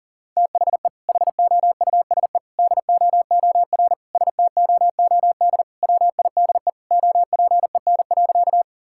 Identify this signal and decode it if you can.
Morse code, heard as a keyed tone, about 40 wpm, 700 Hz. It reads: THE HOUSE DOOR STOOD WIDE OPEN.
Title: Sentences from Top 1000 Words - Review Part 6 - 40wpm